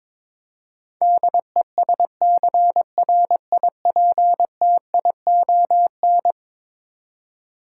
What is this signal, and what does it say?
Morse code, heard as a keyed tone, about 22 wpm, 700 Hz. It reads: DESCRIPTION
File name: practice / 22wpm / Eleven Letter Words - 22wpm